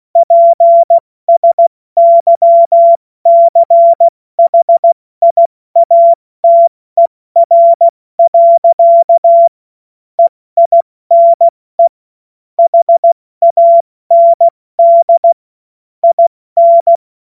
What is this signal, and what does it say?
Morse code, heard as a keyed tone, about 16 wpm, spaced as for 14 wpm, 680 Hz. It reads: PSYCHIATER. EINE HAND IN
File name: Morsekurs / Witze / Witz 19 März 2024 70 F80